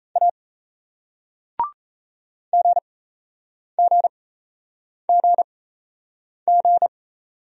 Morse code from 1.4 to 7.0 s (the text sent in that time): GGGG